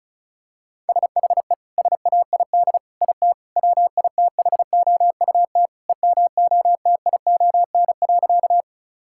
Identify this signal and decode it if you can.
Morse code, heard as a keyed tone, about 34 wpm, 700 Hz. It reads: SHE SAID IT WITHOUT EMOTION.
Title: Sentences from Top 1600 Words 35wpm